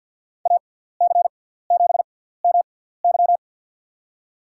Morse code, heard as a keyed tone, about 52 wpm, spaced as for 17 wpm, 700 Hz. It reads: AC7MY
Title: USA Call Signs - 6x Character Spacing - 50wpm